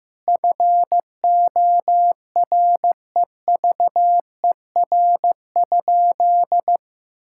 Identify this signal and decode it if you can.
Morse code, heard as a keyed tone, about 15 wpm, 700 Hz. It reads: FOREVER?
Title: Sentences from Top 1600 Words 15wpm